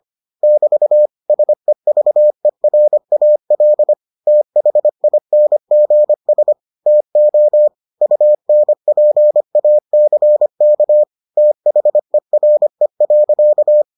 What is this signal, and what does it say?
Morse code, heard as a keyed tone, about 25 wpm, 600 Hz. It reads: <BT> SEVERAL THINGS TO UNPACK THERE.